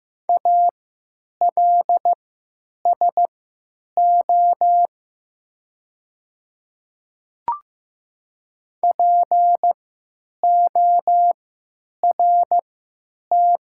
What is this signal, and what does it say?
Morse code, heard as a keyed tone, about 15 wpm, spaced as for 8 wpm, 700 Hz. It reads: ALSO PORT
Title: 3x Character Spacing - Top 200 Words - 15wpm